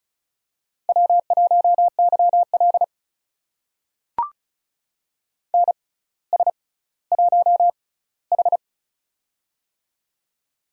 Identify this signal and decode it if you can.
Morse code, heard as a keyed tone, about 36 wpm, spaced as for 18 wpm, 700 Hz. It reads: W1YL NS1H
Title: USA Call Signs - 4 Characters - 6x Character Spacing - 35wpm